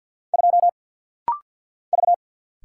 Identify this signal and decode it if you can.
Morse code, heard as a keyed tone, about 49 wpm, 700 Hz. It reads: 2 V
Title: Single Letter-Number Alt - Rapid-Fire 50wpm